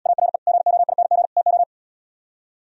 Morse code, heard as a keyed tone, about 80 wpm, 690 Hz. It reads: ARE GOING TO